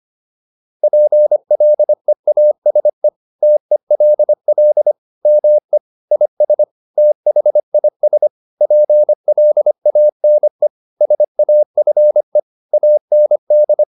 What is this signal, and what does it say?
Morse code, heard as a keyed tone, about 25 wpm, 600 Hz. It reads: PLEASE TELL ME IS THIS PLANE SAFE AND